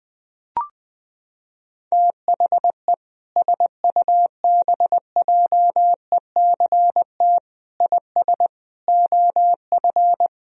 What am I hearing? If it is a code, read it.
Morse code, heard as a keyed tone, about 21 wpm, 700 Hz. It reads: THE SUBJECT IS OF
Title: Sentences from Top 1000 Words - Review Part 4 - 20wpm